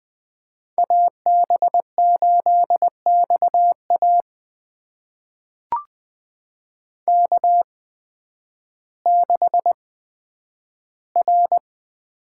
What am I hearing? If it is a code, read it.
Morse code, heard as a keyed tone, about 20 wpm, spaced as for 11 wpm, 700 Hz. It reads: AB8XA K6R
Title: USA Call Signs - 5 Characters - 8x Character Spacing - 20wpm